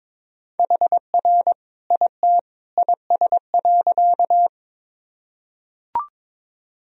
Morse code, heard as a keyed tone, about 22 wpm, 700 Hz. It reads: HR IT IS.
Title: Sentences from Top 200 Words and Abbreviations 22wpm